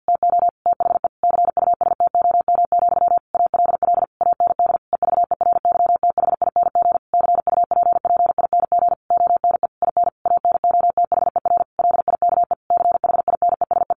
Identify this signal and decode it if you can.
Morse code, heard as a keyed tone, about 50 wpm, 700 Hz. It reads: TO THE CUSTOM, ALL AND EVERYTHING CUPPING ONE IN ANOTHER LIKE CHINESE
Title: A Sound Of Thunder 50wpm